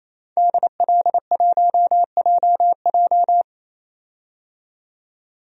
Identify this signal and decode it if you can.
Morse code, heard as a keyed tone, about 27 wpm, 700 Hz. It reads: DL1JJ
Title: DXCC-100 Call Signs - 5 Characters - 28wpm